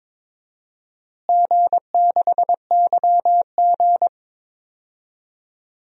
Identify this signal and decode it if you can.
Morse code, heard as a keyed tone, about 21 wpm, 700 Hz. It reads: G6YG
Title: DXCC-100 Call Signs - 4 Characters - Repeated 2x - 22wpm